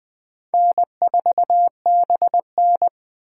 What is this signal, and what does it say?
Morse code, heard as a keyed tone, about 21 wpm, 700 Hz. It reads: N4BN
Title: USA Call Signs - 2x Character Spacing - 20wpm